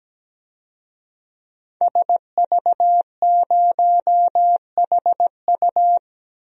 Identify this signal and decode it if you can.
Morse code, heard as a keyed tone, about 17 wpm, 700 Hz. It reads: SV0HU
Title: DXCC-100 Call Signs - 5 Characters - 17wpm